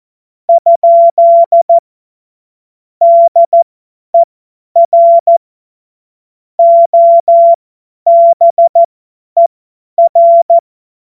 Morse code, heard as a keyed tone, about 14 wpm, spaced as for 10 wpm, 680 Hz. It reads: ? DER OBER
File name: Morsekurs / Witze / Witz 07. September 2021, 50 F70